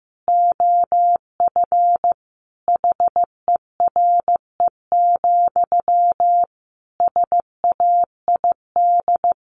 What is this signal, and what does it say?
Morse code, heard as a keyed tone, about 15 wpm, 700 Hz. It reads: OF HERE, SAID
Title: A Sound Of Thunder 15wpm